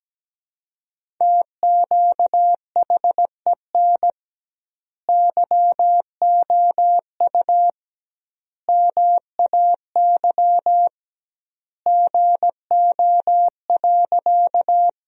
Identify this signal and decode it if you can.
Morse code, heard as a keyed tone, about 17 wpm, 700 Hz. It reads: TQHEN YOU MAY GO.
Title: 2x Word-Spacing - Sentences from Top 100 Words - 17wpm